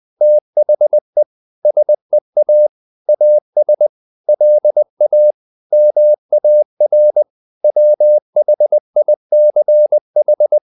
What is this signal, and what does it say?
Morse code, heard as a keyed tone, about 20 wpm, 600 Hz. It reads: THE SEA AS LA MAR WHICH
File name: the old man and the sea / the old man and the sea part 1 20wpm